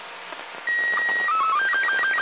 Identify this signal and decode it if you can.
SSTV (slow-scan television), Robot 36.
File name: ao-51-apollo-11